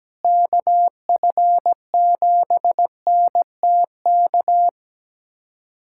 Morse code, heard as a keyed tone, about 17 wpm, 700 Hz. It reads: KF7NTK
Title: USA Callsign DE Callsign - 6 Characters - 17wpm